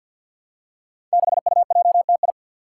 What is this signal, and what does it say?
Morse code, heard as a keyed tone, about 49 wpm, 700 Hz. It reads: 6UJTI
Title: Five Challenging Letter-Number Combinations 50wpm